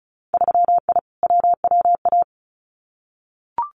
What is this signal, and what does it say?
Morse code, heard as a keyed tone, about 36 wpm, 700 Hz. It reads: 3I WWA
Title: Winter Field Day Exchanges Encore 35wpm